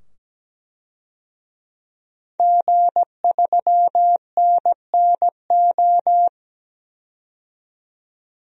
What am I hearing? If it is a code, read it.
Morse code, heard as a keyed tone, about 17 wpm, 700 Hz. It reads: G3NNO
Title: DXCC-100 Call Signs - 5 Characters - 17wpm